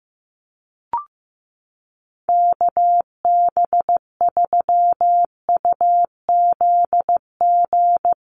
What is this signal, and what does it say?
Morse code, heard as a keyed tone, about 15 wpm, 700 Hz. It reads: KB3UZG
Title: USA Call Signs 15wpm